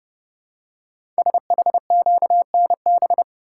Morse code, heard as a keyed tone, about 30 wpm, 700 Hz. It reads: SHQNB
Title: Code Groups - 5 Letters - 30wpm